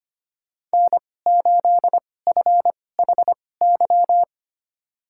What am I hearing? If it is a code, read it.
Morse code, heard as a keyed tone, about 25 wpm, spaced as for 18 wpm, 700 Hz. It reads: N8FHY
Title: USA Call Signs - 5 Characters - 2x Character Spacing - 25wpm